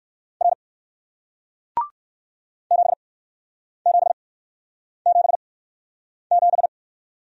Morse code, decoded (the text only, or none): R 7777